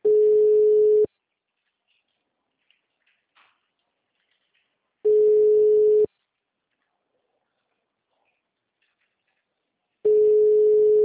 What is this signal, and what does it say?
Ringback tone (Europe, 425 Hz).